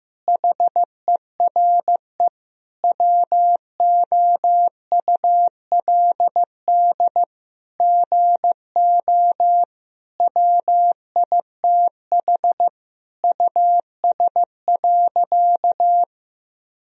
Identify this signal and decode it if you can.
Morse code, heard as a keyed tone, about 15 wpm, 700 Hz. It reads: HERE WOULD GO WITH US.
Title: Sentences from Top 1000 Words - Review Part 3 - 15wpm